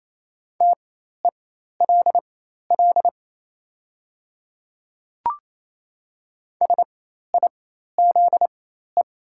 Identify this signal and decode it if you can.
Morse code, heard as a keyed tone, about 28 wpm, spaced as for 13 wpm, 700 Hz. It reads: TELL SIZE
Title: 4x Character Spacing - Top 500 Words - 28wpm